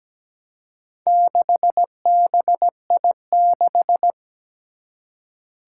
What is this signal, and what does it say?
Morse code, heard as a keyed tone, about 17 wpm, 700 Hz. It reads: 6BI6
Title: Four Challenging Letter-Number Combinations 17wpm